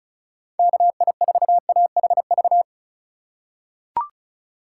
Morse code, heard as a keyed tone, about 36 wpm, 700 Hz. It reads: KI4AHV
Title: USA Call Signs - 2x Character Spacing - 35wpm